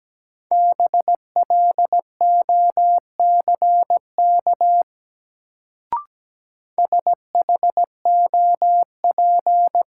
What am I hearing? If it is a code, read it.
Morse code, heard as a keyed tone, about 17 wpm, 700 Hz. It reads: BLOCK SHOP
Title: Sentences from Top 1600 Words - New Word Drill - 17wpm